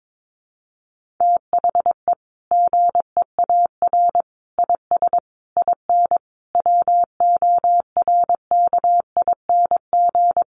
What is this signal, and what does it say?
Morse code, heard as a keyed tone, about 22 wpm, 700 Hz. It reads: THE GEAR IS IN WORKING
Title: Sentences from Top 1400 Words 22wpm